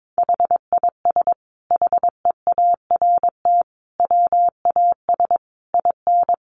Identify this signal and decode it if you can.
Morse code, heard as a keyed tone, about 22 wpm, 700 Hz. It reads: HIS HEART WAS IN
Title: Sentences from Top 1400 Words 22wpm